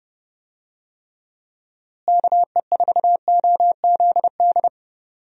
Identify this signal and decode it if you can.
Morse code, heard as a keyed tone, about 32 wpm, 700 Hz. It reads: KE4OZD
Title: USA Call Signs - 6 Characters - Repeated 3x - 30wpm